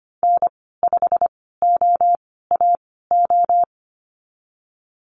Morse code, heard as a keyed tone, about 25 wpm, spaced as for 16 wpm, 700 Hz. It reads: N5OAO
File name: Lesson 15g - Number 5 - Call Signs 25wpm